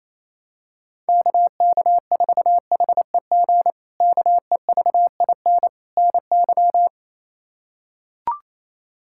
Morse code, heard as a keyed tone, about 28 wpm, 700 Hz. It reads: KK4HEG KEVIN NY
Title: K1USN SST QSO Practice - 28wpm